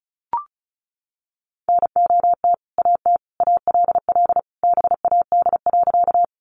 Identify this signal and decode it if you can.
Morse code, heard as a keyed tone, about 34 wpm, 700 Hz. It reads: NOT AT ALL BAD.
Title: Sentences from Top 700 Words 35wpm